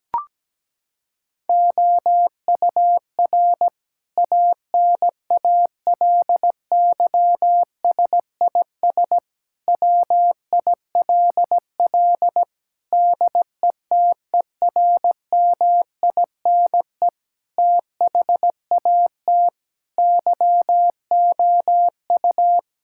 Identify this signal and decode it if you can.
Morse code, heard as a keyed tone, about 17 wpm, 700 Hz. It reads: OUR ANALYSIS WILL DETERMINE THAT YOU
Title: Sentences from Top 1000 Words - Review Part 5 - 17wpm